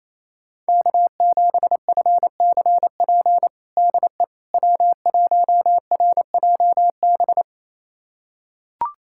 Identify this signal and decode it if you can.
Morse code, heard as a keyed tone, about 28 wpm, 700 Hz. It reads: K7FCP DE W1RJB
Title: USA Callsign DE Callsign - 5 Characters - 28wpm